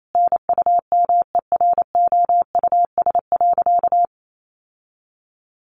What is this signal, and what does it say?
Morse code, heard as a keyed tone, about 28 wpm, 700 Hz. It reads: NUMEROUS.
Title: Sentences from Top 1400 Words 28wpm